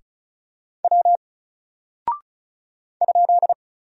Morse code, heard as a keyed tone, about 36 wpm, 700 Hz. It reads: W ?
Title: Single Character (Letter:Number:Punctuation:Prosign) 35wpm